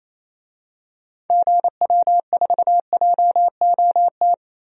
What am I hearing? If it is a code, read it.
Morse code, heard as a keyed tone, about 28 wpm, 700 Hz. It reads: GW4JOT
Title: DXCC-100 Call Signs - 6 Characters - 28wpm